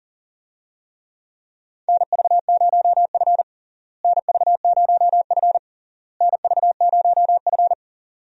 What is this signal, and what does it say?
Morse code, heard as a keyed tone, about 40 wpm, 700 Hz. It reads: NV0F NV0F NV0F
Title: USA Call Signs - 4 Characters - Repeated 3x - 40wpm